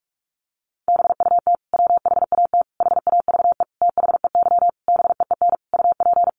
Morse code, heard as a keyed tone, about 44 wpm, 700 Hz. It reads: BUT WHAT HAVE THEY BEEN UP
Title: Sentences from Top 100 Words 45wpm